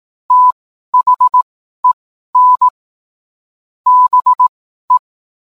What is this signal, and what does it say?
Morse code, heard as a keyed tone, about 18 wpm, spaced as for 12 wpm, 1000 Hz. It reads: THEN BE